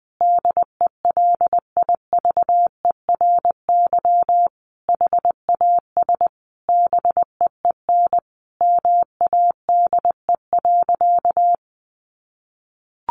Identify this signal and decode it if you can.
Morse code, heard as a keyed tone, about 20 wpm, 700 Hz. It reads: DELIVERY HAS BEEN MADE.
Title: Sentences from Top 1500 Words 20wpm